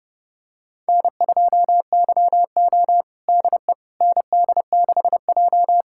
Morse code, heard as a keyed tone, about 30 wpm, 700 Hz. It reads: N2YO DE ND6J
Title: USA Callsign DE Callsign - 4 Characters - 30wpm